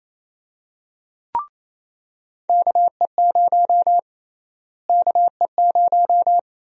Morse code, heard as a keyed tone, about 28 wpm, 700 Hz. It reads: KE0 KE0
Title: USA Call Sign Prefixes - Repeated 2x - 28wpm